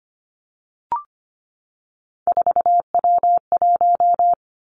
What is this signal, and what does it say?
Morse code, heard as a keyed tone, about 25 wpm, 700 Hz. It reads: 4W1
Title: Three Challenging Letter-Number Combinations 25wpm